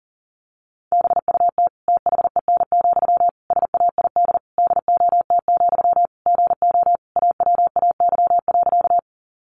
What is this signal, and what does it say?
Morse code, heard as a keyed tone, about 40 wpm, 700 Hz. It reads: BUT THEN, SAID DOT, GO AWAY.
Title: Sentences from Top 1500 Words 40wpm